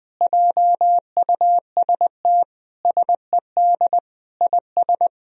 Morse code, heard as a keyed tone, about 20 wpm, 700 Hz. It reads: JUST SED IS
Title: Sentences from Top 200 Words and Abbreviations 20wpm